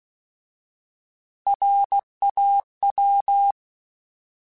Morse code, heard as a keyed tone, about 16 wpm, 800 Hz. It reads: RAW
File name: Alphabetical / Alphabetical mp3s / Rs @15